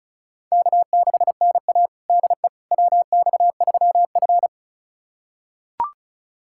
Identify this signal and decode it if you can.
Morse code, heard as a keyed tone, about 34 wpm, 700 Hz. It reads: K6NA DE WX3F
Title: USA Callsign DE Callsign - 4 Characters - 35wpm